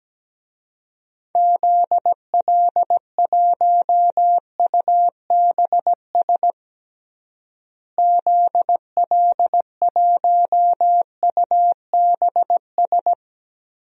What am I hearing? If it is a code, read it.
Morse code, heard as a keyed tone, about 17 wpm, 700 Hz. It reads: ZL1UBS ZL1UBS
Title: DXCC-100 Call Signs - 6 Characters - Repeated 3x - 17wpm